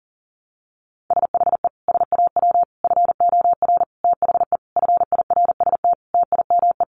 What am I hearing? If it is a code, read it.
Morse code, heard as a keyed tone, about 40 wpm, 700 Hz. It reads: SHE SAW FOR THE FIRST TIME
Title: Sentences from Top 1200 Words 40wpm